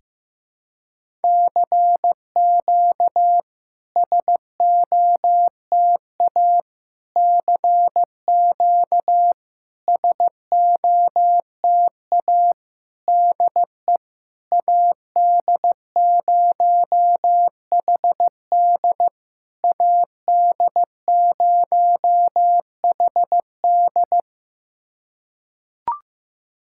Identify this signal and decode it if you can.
Morse code, heard as a keyed tone, about 15 wpm, 700 Hz. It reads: CQ SOTA CQ SOTA DE AD0HD AD0HD
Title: SOTA QSO Practice 15wpm